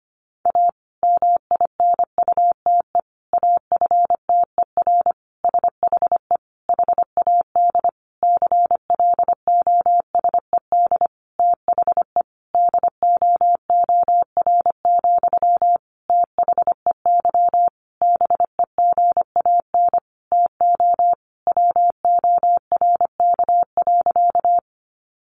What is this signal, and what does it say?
Morse code, heard as a keyed tone, about 25 wpm, 700 Hz. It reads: A MINUTE AFTER SHE HAD CLOSED THE DOOR, THEY BEGAN TO WORK.
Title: Sentences from Top 1400 Words 25wpm